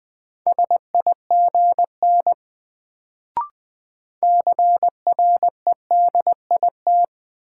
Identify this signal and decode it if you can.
Morse code, heard as a keyed tone, about 20 wpm, 700 Hz. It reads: SIGN CREDIT
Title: Sentences from Top 1000 Words - New Word Drill - 20wpm